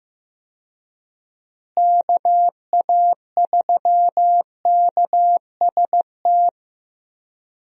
Morse code, heard as a keyed tone, about 15 wpm, 700 Hz. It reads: KA3KST
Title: USA Call Signs - 6 Characters - Repeated 3x - 15wpm